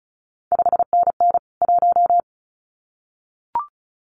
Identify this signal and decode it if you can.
Morse code, heard as a keyed tone, about 36 wpm, 700 Hz. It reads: 5NN 1
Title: CQ WW DX Contest Exchange Encore - 35wpm